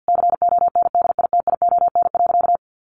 Morse code, heard as a keyed tone, about 50 wpm, 700 Hz. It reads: CONDITION.
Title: Sentences from Top 1200 Words 50wpm